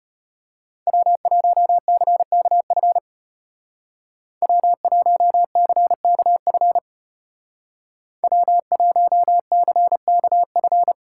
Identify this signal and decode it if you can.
Morse code, heard as a keyed tone, about 31 wpm, 700 Hz. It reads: W1CKF W1CKF W1CKF
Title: Speed Racing - USA 1x3 Call Signs 25wpm